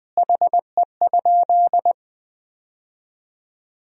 Morse code, heard as a keyed tone, about 20 wpm, 700 Hz. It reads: HE?
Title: 3x Word-Spacing - Sentences from Top 200 Words - 20wpm